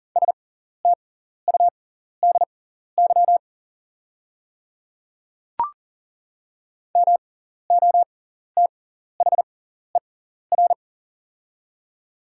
Morse code, heard as a keyed tone, about 40 wpm, spaced as for 14 wpm, 690 Hz. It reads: STUDY MOTHER
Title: 6x Character Spacing - Top 300 Words - 40wpm